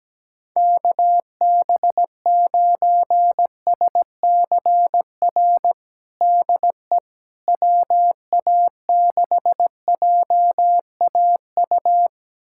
Morse code, heard as a keyed tone, about 17 wpm, 700 Hz. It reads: KB9SCR DE WA6JAU
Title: USA Callsign DE Callsign - 6 Characters - 17wpm